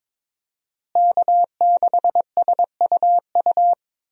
Morse code, heard as a keyed tone, about 22 wpm, 700 Hz. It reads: K6SUU
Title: USA Call Signs - 5 Characters - Repeated 3x - 22wpm